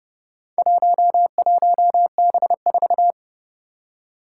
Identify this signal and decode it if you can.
Morse code, heard as a keyed tone, about 29 wpm, 690 Hz. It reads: 11B4
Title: Four Challenging Letter-Number Combinations 30wpm